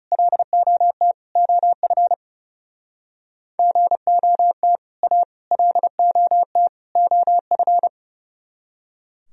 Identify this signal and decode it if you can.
Morse code, heard as a keyed tone, about 38 wpm, 700 Hz. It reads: LOT OF GOT A LOT OF